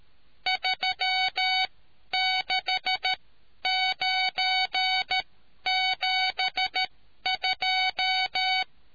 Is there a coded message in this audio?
Yes: Morse code from 0.5 to 8.6 s, heard as a keyed tone, about 13 wpm, spaced as for 10 wpm, 750 Hz. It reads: 36972